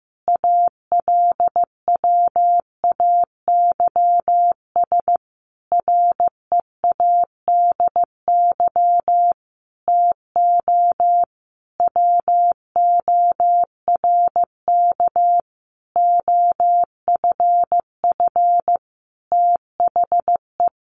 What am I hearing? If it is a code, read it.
Morse code, heard as a keyed tone, about 15 wpm, 700 Hz. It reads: ALWAYS READY TO WORK OFF THE
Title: Sentences from Top 1300 Words 15wpm